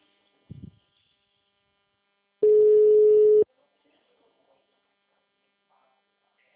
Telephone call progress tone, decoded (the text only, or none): ringback tone